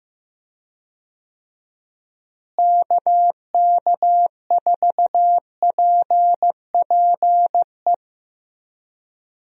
Morse code, heard as a keyed tone, about 15 wpm, 700 Hz. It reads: KK4PPE
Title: USA Call Signs - 6 Characters - Repeated 3x - 15wpm